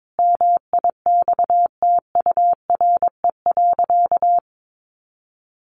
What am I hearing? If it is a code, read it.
Morse code, heard as a keyed tone, about 22 wpm, 700 Hz. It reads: MIXTURE.
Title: Sentences from Top 1500 Words 22wpm